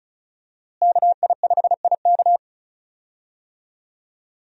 Morse code, heard as a keyed tone, about 34 wpm, 700 Hz. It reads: KI5IK
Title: USA Call Signs - 5 Characters - Repeated 2x - 35wpm